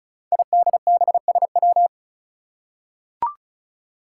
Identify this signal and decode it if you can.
Morse code, heard as a keyed tone, about 36 wpm, 700 Hz. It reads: IDBSW
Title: Five Challenging Letter-Number Combinations 35wpm